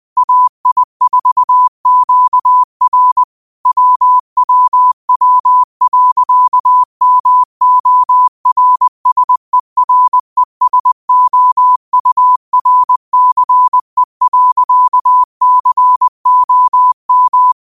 Morse code, heard as a keyed tone, about 21 wpm, 1000 Hz. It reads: AI4QR WWW.MORSERESOURCE.COM
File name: qotdfiles / 20wpm